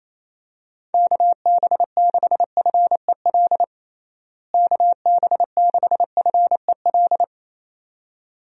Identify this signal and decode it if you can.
Morse code, heard as a keyed tone, about 28 wpm, 700 Hz. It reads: KB6FEL KB6FEL
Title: USA Call Signs - 6 Characters - Repeated 2x - 28wpm